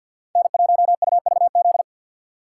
Morse code, heard as a keyed tone, about 52 wpm, 700 Hz. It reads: N1UVZ
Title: USA Call Signs - 5 Characters - 3x Character Spacing - 50wpm